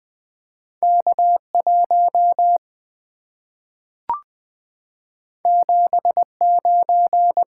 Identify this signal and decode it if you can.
Morse code, heard as a keyed tone, about 20 wpm, 690 Hz. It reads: K1 79
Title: Two Letter-Number Combinations - 20wpm